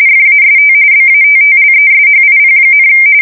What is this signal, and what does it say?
RTTY, about 45.45 baud. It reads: FAST BATTLESHIP,
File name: fsk170 / USS-New-Jersey